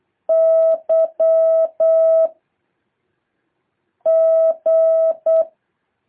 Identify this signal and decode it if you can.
Morse code, heard as a keyed tone, about 8 wpm, 650 Hz. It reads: Y G
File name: jh8cbh / 2023cw